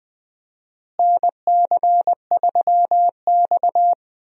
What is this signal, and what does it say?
Morse code, heard as a keyed tone, about 20 wpm, 700 Hz. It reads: NC3X
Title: USA Call Signs - 4 Characters - Repeated 2x - 20wpm